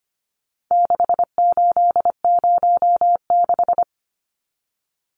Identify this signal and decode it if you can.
Morse code, heard as a keyed tone, about 25 wpm, 700 Hz. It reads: 6806
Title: Numbers - 4-Digit 25wpm